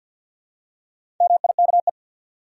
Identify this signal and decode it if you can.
Morse code, heard as a keyed tone, about 49 wpm, 700 Hz. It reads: MIKE
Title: USA Common Names - Max 5 Letters - 50wpm